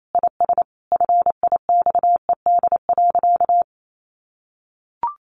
Morse code, heard as a keyed tone, about 28 wpm, 700 Hz. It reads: IS FIXED.
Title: Sentences from Top 1100 Words 28wpm